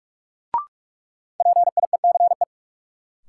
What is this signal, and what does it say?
Morse code, heard as a keyed tone, about 41 wpm, 700 Hz. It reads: PIECE
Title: Sentences from Top 400 Words - New Word Drill - 45wpm